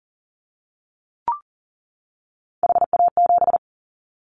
Morse code, heard as a keyed tone, about 38 wpm, 700 Hz. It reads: HA7
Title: Callsign Prefixes for Top 25 DXCC - 40wpm